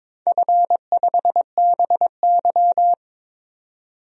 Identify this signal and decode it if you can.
Morse code, heard as a keyed tone, about 22 wpm, 700 Hz. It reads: F5BY
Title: DXCC-100 Call Signs - 4 Characters - 22wpm